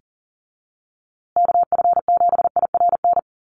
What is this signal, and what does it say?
Morse code, heard as a keyed tone, about 40 wpm, 700 Hz. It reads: KF7IRN
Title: USA Call Signs Repeated 2x 40wpm